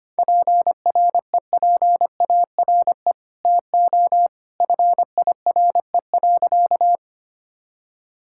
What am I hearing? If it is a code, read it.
Morse code, heard as a keyed tone, about 25 wpm, 700 Hz. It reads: PREPARE TO FIRE.
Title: Sentences from Top 1000 Words - Review Part 6 - 25wpm